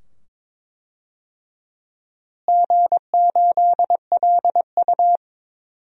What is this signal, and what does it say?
Morse code, heard as a keyed tone, about 22 wpm, 700 Hz. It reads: G8LU
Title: DXCC-100 Call Signs - 4 Characters - Repeated 2x - 22wpm